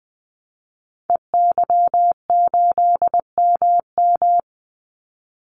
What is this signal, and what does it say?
Morse code, heard as a keyed tone, about 19 wpm, 700 Hz. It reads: EY8MM
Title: Call Signs Repeated 2x - 20wpm